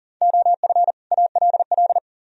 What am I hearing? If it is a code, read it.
Morse code, heard as a keyed tone, about 51 wpm, 700 Hz. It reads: OF ALL